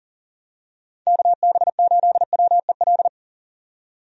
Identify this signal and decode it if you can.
Morse code, heard as a keyed tone, about 39 wpm, 700 Hz. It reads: KB8WEL
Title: USA Call Signs - 6 Characters - 40wpm